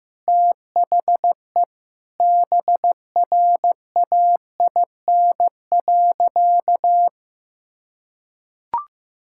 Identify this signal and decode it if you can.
Morse code, heard as a keyed tone, about 14 wpm, 700 Hz. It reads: THE BRAIN.
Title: Sentences from Top 1000 Words - Review Part 6 - 15wpm